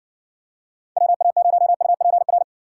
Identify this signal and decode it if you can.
Morse code, heard as a keyed tone, about 60 wpm, 700 Hz. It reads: WA0UPR